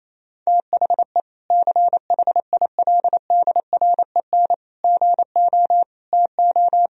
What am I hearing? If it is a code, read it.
Morse code, heard as a keyed tone, about 28 wpm, 700 Hz. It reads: THE CHILDREN GO TO
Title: Sentences from Top 1000 Words - Review Part 1 - 28wpm